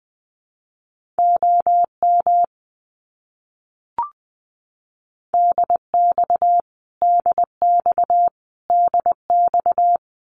Morse code, heard as a keyed tone, about 20 wpm, 700 Hz. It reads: OM DX DX DX
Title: Abbreviations Repeated 3x - 20wpm